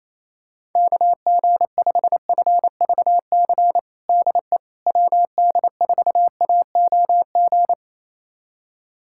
Morse code, heard as a keyed tone, about 28 wpm, 700 Hz. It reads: KG5FVC DE WD4AOG
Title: USA Callsign DE Callsign 28wpm